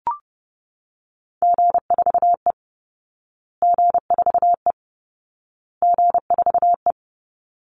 Morse code, heard as a keyed tone, about 30 wpm, 700 Hz. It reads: G4E G4E G4E
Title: Call Signs Repeated 2x - 30wpm